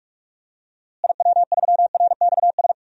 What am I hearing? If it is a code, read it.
Morse code, heard as a keyed tone, about 42 wpm, 700 Hz. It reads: IW3RXS
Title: DXCC-100 Call Signs - 6 Characters - Repeated 3x - 45wpm